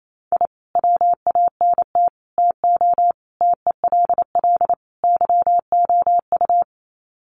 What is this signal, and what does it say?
Morse code, heard as a keyed tone, about 28 wpm, 700 Hz. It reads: I WANT TO TELL YOU
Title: Sets of 5 Words Encore 28wpm